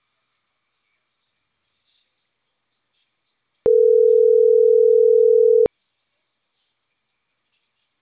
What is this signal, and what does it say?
Ringback tone (North America), 440 + 480 Hz.